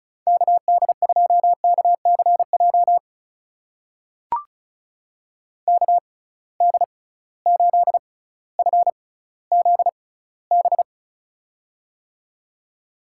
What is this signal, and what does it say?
Morse code, heard as a keyed tone, about 36 wpm, spaced as for 18 wpm, 690 Hz. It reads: KD2KCJ KD8FZB
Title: USA Call Signs - 6 Characters - 6x Character Spacing - 35wpm